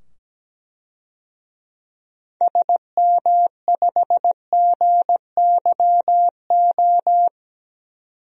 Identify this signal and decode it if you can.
Morse code, heard as a keyed tone, about 17 wpm, 700 Hz. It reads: SM5GYO
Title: DXCC-100 Call Signs - 6 Characters - 17wpm